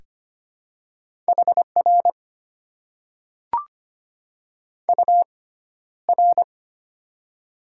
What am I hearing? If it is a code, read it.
Morse code, heard as a keyed tone, about 25 wpm, spaced as for 13 wpm, 690 Hz. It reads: HR UR
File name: Top 90 QSO Elements - 6x Character Spacing - 25wpm